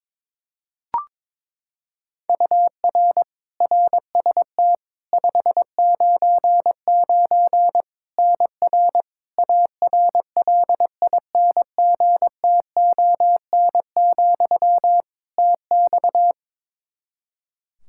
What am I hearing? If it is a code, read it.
Morse code, heard as a keyed tone, about 21 wpm, 700 Hz. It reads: UR RST 599 NR ARLINGTON, TX
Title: Practice QSO - Exchange 1 - Beginning - 22wpm